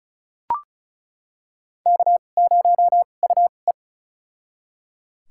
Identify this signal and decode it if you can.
Morse code, heard as a keyed tone, about 34 wpm, spaced as for 25 wpm, 700 Hz. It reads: K0UE
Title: USA Call Signs - 4 Characters - 2x Character Spacing - 35wpm